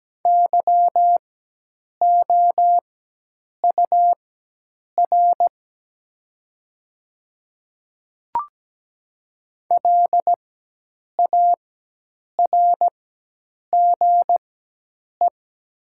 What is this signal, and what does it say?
Morse code, heard as a keyed tone, about 17 wpm, spaced as for 8 wpm, 700 Hz. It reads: YOUR LARGE
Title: 4x Character Spacing - Top 300 Words - 17wpm